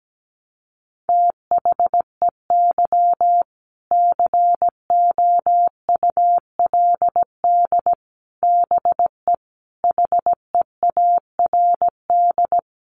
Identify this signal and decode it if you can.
Morse code, heard as a keyed tone, about 17 wpm, 700 Hz. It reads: THEY COULD BE HEARD
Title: Sentences from Top 1200 Words 17wpm